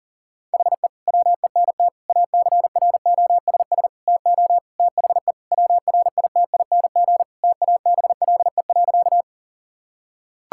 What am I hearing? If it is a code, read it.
Morse code, heard as a keyed tone, about 40 wpm, 700 Hz. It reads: HE WENT ACROSS TO THE WRITING TABLE.
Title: Sentences from Top 1000 Words - Review Part 5 - 40wpm